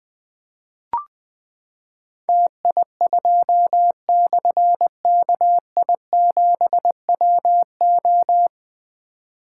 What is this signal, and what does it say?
Morse code, heard as a keyed tone, about 21 wpm, 700 Hz. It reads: TI2/KI7WO
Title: Difficult Call Signs 20wpm